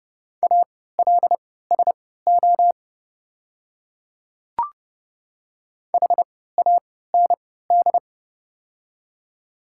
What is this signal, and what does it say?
Morse code, heard as a keyed tone, about 30 wpm, spaced as for 17 wpm, 700 Hz. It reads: ALSO HAND
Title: 3x Character Spacing - Top 300 Words - 30wpm